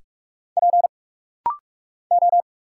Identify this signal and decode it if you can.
Morse code, heard as a keyed tone, about 46 wpm, 700 Hz. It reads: P O
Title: Single Letters - Rapid-Fire 45wpm